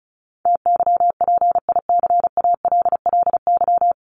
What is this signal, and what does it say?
Morse code, heard as a keyed tone, about 34 wpm, 700 Hz. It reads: TYPICALLY
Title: LY Suffix - Top 100 Words with Base Words 35wpm